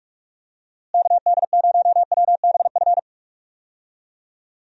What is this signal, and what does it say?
Morse code, heard as a keyed tone, about 46 wpm, 700 Hz. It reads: KD0WBF
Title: USA Call Signs - 6 Characters - Repeated 3x - 45wpm